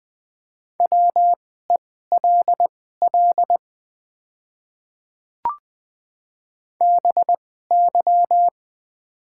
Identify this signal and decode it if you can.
Morse code, heard as a keyed tone, about 20 wpm, spaced as for 15 wpm, 700 Hz. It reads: WELL BY
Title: 2x Character Spacing - Top 200 Words - 20wpm